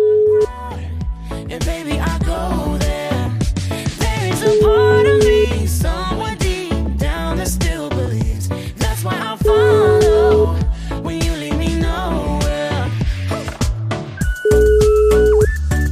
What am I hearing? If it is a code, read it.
Ringback tone (Europe, 425 Hz), with noise.